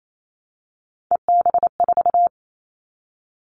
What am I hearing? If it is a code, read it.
Morse code, heard as a keyed tone, about 28 wpm, 700 Hz. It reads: EB4
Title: Callsign Prefixes for Top 25 DXCC - 28wpm